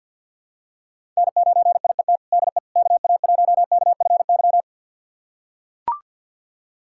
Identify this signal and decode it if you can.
Morse code, heard as a keyed tone, about 45 wpm, 700 Hz. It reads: N9IET DE KA1KFY
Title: USA Callsign DE Callsign 50wpm